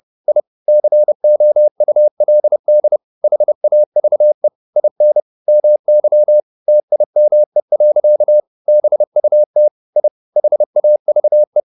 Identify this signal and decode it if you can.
Morse code, heard as a keyed tone, about 30 wpm, 600 Hz. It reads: I COULD HAVE IN MY TIME. BUT I HAVE